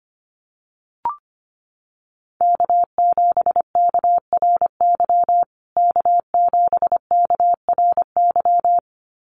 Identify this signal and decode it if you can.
Morse code, heard as a keyed tone, about 25 wpm, 700 Hz. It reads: K7KRY K7KRY
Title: USA Call Signs Repeated 2x 25wpm